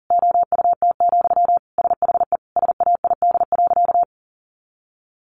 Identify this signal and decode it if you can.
Morse code, heard as a keyed tone, about 40 wpm, 700 Hz. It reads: OUT, SHE SAID.
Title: Sentences from Top 500 Words 40wpm